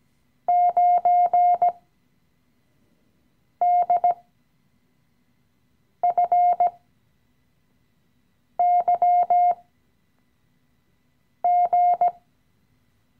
Morse code, heard as a keyed tone, about 17 wpm, spaced as for 4 wpm, 710 Hz. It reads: 9DFYG
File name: CW-Online-Kurs / Anfänger / Lektion 015